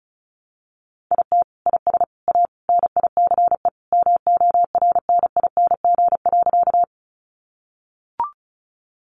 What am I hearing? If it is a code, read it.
Morse code, heard as a keyed tone, about 34 wpm, 700 Hz. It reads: IT IS A NICE MORNING.
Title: Sentences from Top 1100 Words 35wpm